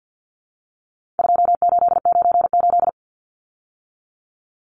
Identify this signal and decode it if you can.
Morse code, heard as a keyed tone, about 50 wpm, 700 Hz. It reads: 2898
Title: Numbers - 4-Digit 50wpm